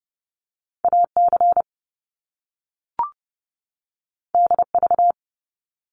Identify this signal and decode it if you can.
Morse code, heard as a keyed tone, about 30 wpm, 700 Hz. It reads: AC DV
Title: Top Two Letter Combinations in Words 30wpm